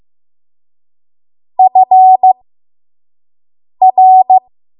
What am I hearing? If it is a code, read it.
Morse code, heard as a keyed tone, about 15 wpm, spaced as for 5 wpm, 750 Hz. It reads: FR
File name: Morse / 05WPM